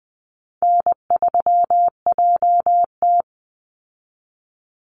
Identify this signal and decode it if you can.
Morse code, heard as a keyed tone, about 20 wpm, 700 Hz. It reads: N3JT
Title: Easy Call Signs Repeated 2x - 20wpm